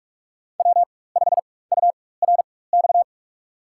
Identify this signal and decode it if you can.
Morse code, heard as a keyed tone, about 46 wpm, spaced as for 21 wpm, 700 Hz. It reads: W5URX
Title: USA Call Signs - 5 Characters - 4x Character Spacing - 45wpm